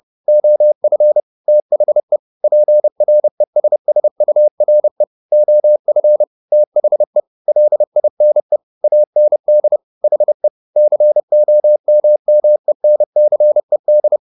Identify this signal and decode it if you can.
Morse code, heard as a keyed tone, about 30 wpm, 600 Hz. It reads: OF THE PRESSURE OF THE LINE AND HE COMMENCED